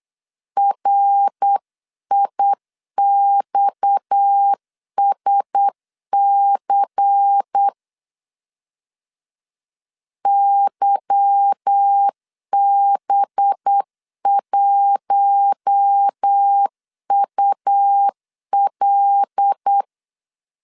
Morse code, heard as a keyed tone, about 8 wpm, 800 Hz. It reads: RIXSC YB1UL
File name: CW / DET / 1 9 1500C S B 8wpm 5624